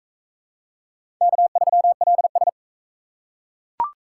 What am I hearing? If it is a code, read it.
Morse code, heard as a keyed tone, about 38 wpm, 700 Hz. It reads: K3LS